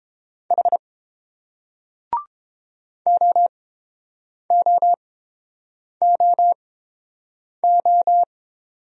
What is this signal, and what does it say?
Morse code, heard as a keyed tone, about 30 wpm, spaced as for 7 wpm, 700 Hz. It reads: H OOOO